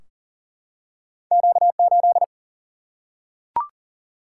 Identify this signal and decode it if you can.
Morse code, heard as a keyed tone, about 40 wpm, 700 Hz. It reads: Q8
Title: Single-Letter Single-Number 40wpm